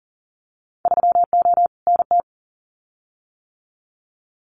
Morse code, heard as a keyed tone, about 39 wpm, 700 Hz. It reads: 3O NT
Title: Winter Field Day Exchanges Encore 40wpm